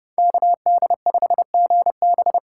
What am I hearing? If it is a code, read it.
Morse code, heard as a keyed tone, about 33 wpm, 700 Hz. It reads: KD5GB